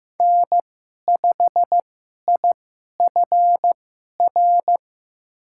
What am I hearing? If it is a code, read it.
Morse code, heard as a keyed tone, about 15 wpm, spaced as for 11 wpm, 700 Hz. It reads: N5IFR
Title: USA Call Signs - 5 Characters - 2x Character Spacing - 15wpm